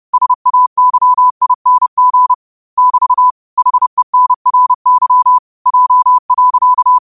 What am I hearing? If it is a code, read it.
Morse code, heard as a keyed tone, about 30 wpm, 1000 Hz. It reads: SAYING <BT> HENRY J.
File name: qotdfiles / 30wpm